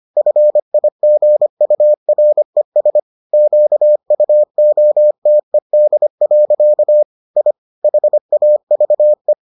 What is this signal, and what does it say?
Morse code, heard as a keyed tone, about 25 wpm, 600 Hz. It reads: FIGURES QUOTED. I HAVE